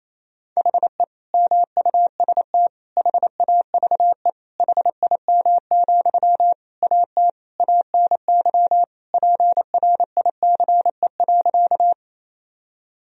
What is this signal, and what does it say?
Morse code, heard as a keyed tone, about 28 wpm, 700 Hz. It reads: HE MUST HAVE HIM, AT ANY PRICE.
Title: Sentences from Top 1000 Words - Review Part 4 - 28wpm